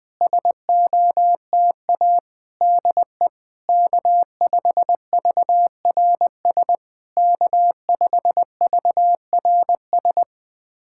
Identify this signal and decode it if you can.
Morse code, heard as a keyed tone, about 21 wpm, 700 Hz. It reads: SOTA DE K5VRS K5VRS
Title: SOTA QSO Practice 20wpm